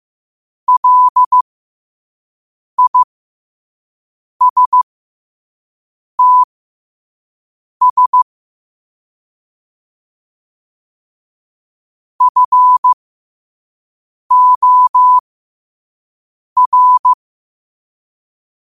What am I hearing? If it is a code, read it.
Morse code, heard as a keyed tone, about 15 wpm, spaced as for 5 wpm, 1000 Hz. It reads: LISTS FOR